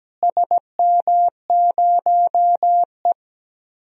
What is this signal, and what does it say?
Morse code, heard as a keyed tone, about 17 wpm, 700 Hz. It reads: SM0E
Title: DXCC-100 Call Signs - 4 Characters - Repeated 3x - 17wpm